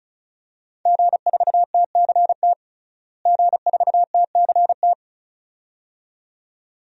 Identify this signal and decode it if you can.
Morse code, heard as a keyed tone, about 36 wpm, 700 Hz. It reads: G4TCT G4TCT
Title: DXCC-100 Call Signs - 5 Characters - Repeated 2x - 35wpm